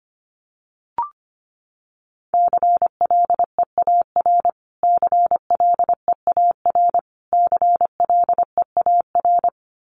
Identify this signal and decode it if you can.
Morse code, heard as a keyed tone, about 25 wpm, 700 Hz. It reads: CLEAR CLEAR CLEAR
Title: Top 500 Words Repeated 25wpm